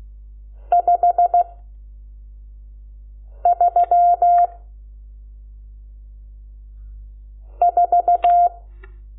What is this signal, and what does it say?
Morse code, heard as a keyed tone, about 16 wpm, 700 Hz. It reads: 5 3 4
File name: Lecciones / CLASE Nº12